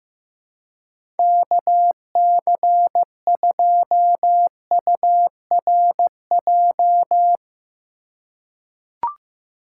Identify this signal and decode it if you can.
Morse code, heard as a keyed tone, about 15 wpm, 700 Hz. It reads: KC2URJ
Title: USA Call Signs - 6x Character Spacing - 15wpm